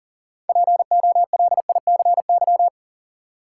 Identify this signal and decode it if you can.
Morse code, heard as a keyed tone, about 40 wpm, 700 Hz. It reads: POLICY
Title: Six Letter Words - 40wpm